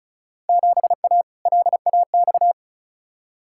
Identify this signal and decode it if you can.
Morse code, heard as a keyed tone, about 34 wpm, 700 Hz. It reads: 7A LAX
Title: Field Day Exchanges - 35wpm